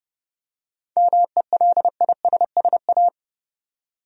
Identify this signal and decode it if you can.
Morse code, heard as a keyed tone, about 31 wpm, 700 Hz. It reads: MELISSA
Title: USA Common Names - 30wpm